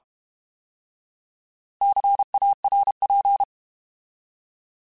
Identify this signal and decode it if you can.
Morse code, heard as a keyed tone, about 32 wpm, 800 Hz. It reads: CARP
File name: Alphabetical / Alphabetical mp3s / Cs @30